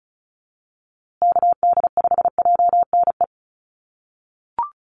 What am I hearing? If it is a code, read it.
Morse code, heard as a keyed tone, about 34 wpm, 700 Hz. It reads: KD5JNE
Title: USA Call Signs Repeated 3x 35wpm